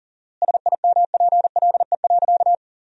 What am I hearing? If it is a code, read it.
Morse code, heard as a keyed tone, about 40 wpm, 700 Hz. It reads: SIMPLE.
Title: 4x Word-Spacing - Sentences from Top 500 Words - 40wpm